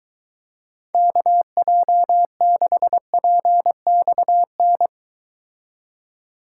Morse code, heard as a keyed tone, about 23 wpm, 700 Hz. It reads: KJ6PXN